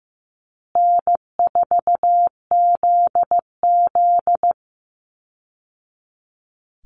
Morse code, heard as a keyed tone, about 15 wpm, 700 Hz. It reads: N4ZZ
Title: Call Signs Repeated 3x - 15wpm